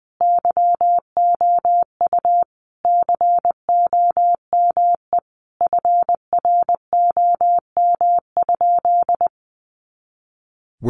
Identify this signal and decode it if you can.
Morse code, heard as a keyed tone, about 20 wpm, 700 Hz. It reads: YOU COME FROM?
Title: Sentences from Top 200 Words 20wpm